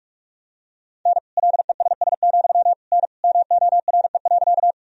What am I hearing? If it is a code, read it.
Morse code, heard as a keyed tone, about 41 wpm, 700 Hz. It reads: N LESS, N MORE.
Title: Sentences from Top 400 Words and Abbreviations 45wpm